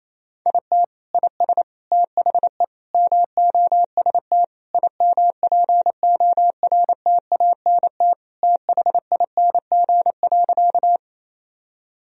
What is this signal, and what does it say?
Morse code, heard as a keyed tone, about 28 wpm, 700 Hz. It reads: IT IS THE MOST IMPORTANT THING.
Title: Sentences from Top 1000 Words - Review Part 1 - 28wpm